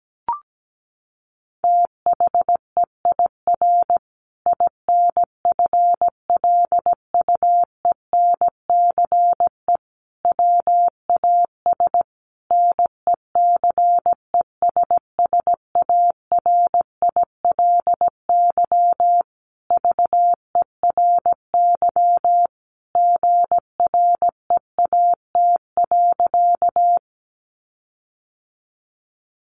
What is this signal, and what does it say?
Morse code, heard as a keyed tone, about 17 wpm, 700 Hz. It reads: THEIR INFLUENCE WAS NECESSARILY VERY GREAT.
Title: Sentences from Top 1400 Words 17wpm